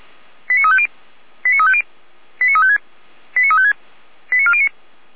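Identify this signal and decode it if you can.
Five-tone selective call (ZVEI), heard as quick runs of five tones, tones of about 70 ms. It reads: 89350 89350 89356 89356 89409
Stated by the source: ZVEI five-tone call received on an SDR, recorded by the Feuerwehr Orlamünde fire brigade.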